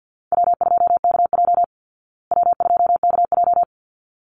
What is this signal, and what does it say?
Morse code, heard as a keyed tone, about 50 wpm, 700 Hz. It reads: W2KJ W2KJ
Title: Call Signs Repeated 3x - 50wpm